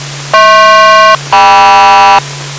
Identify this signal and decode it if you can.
Aircraft SELCAL, heard as two pairs of tones. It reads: HP-KM